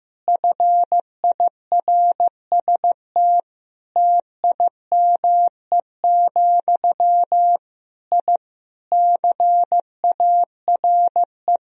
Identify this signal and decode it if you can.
Morse code, heard as a keyed tone, about 15 wpm, 700 Hz. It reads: FIRST TIME, I CARE
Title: Sentences from Top 1000 Words - Review Part 2 - 15wpm